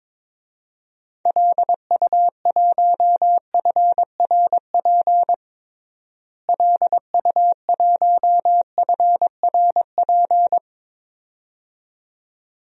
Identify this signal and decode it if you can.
Morse code, heard as a keyed tone, about 22 wpm, 700 Hz. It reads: LU1FRP LU1FRP
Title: DXCC-100 Call Signs - 6 Characters - Repeated 2x - 22wpm